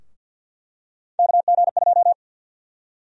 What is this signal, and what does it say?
Morse code, heard as a keyed tone, about 50 wpm, 700 Hz. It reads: KG2
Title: USA Call Sign Prefixes - 50wpm